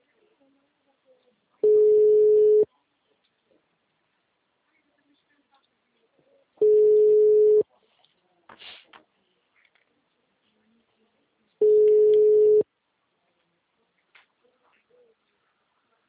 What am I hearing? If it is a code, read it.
Ringback tone (Europe, 425 Hz), with speech.